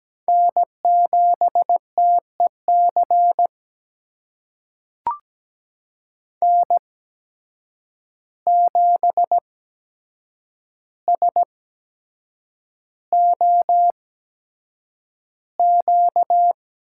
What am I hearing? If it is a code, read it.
Morse code, heard as a keyed tone, about 17 wpm, spaced as for 7 wpm, 700 Hz. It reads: N7TEC N7SOQ
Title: USA Call Signs - 8x Character Spacing - 17wpm